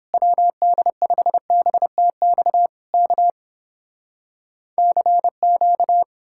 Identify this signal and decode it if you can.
Morse code, heard as a keyed tone, about 32 wpm, 700 Hz. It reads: WD5BTX K CQ